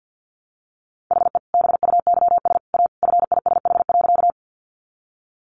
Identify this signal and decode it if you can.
Morse code, heard as a keyed tone, about 48 wpm, 700 Hz. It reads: HE BUYS A FISH.
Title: Easy Phrases 50wpm